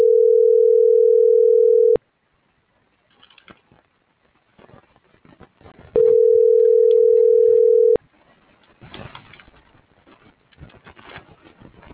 Ringback tone (North America), 440 + 480 Hz.